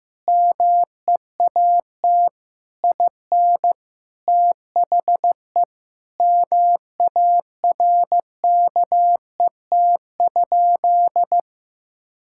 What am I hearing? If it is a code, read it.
Morse code, heard as a keyed tone, about 15 wpm, 700 Hz. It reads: MEAT IN THE MARKET?
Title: Sentences from Top 1000 Words - Review Part 6 - 15wpm